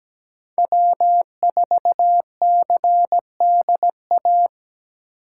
Morse code, heard as a keyed tone, about 20 wpm, 700 Hz. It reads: W4CDA
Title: Speed Racing - USA 1x3 Call Signs 17wpm